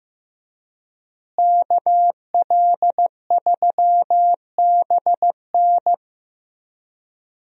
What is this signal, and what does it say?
Morse code, heard as a keyed tone, about 15 wpm, 700 Hz. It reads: KL3BN
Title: USA Call Signs - 5 Characters - Repeated 3x - 15wpm